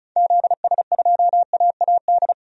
Morse code, heard as a keyed tone, about 34 wpm, 700 Hz. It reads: ZS2AAD
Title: DXCC-100 Call Signs - 6 Characters - 35wpm